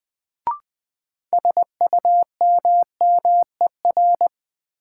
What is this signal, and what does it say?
Morse code, heard as a keyed tone, about 20 wpm, 700 Hz. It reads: SUMMER
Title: Sentences from Top 1000 Words - Review Part 6 - New Word Drill - 20wpm